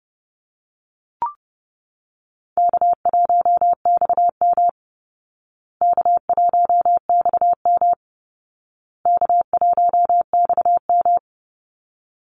Morse code, heard as a keyed tone, about 28 wpm, 700 Hz. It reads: K1XM K1XM K1XM
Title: Call Signs Repeated 2x - 30wpm